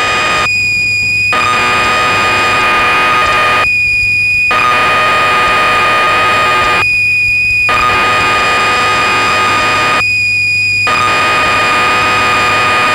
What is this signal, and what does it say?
Cassette data (Kansas City type).
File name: magazines / Acorn User / 1985-03 BBC Programs